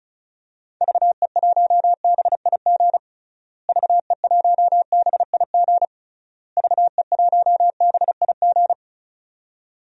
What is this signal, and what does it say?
Morse code, heard as a keyed tone, about 34 wpm, 700 Hz. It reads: VE1BIG VE1BIG VE1BIG
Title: DXCC-100 Call Signs - 6 Characters - Repeated 3x - 35wpm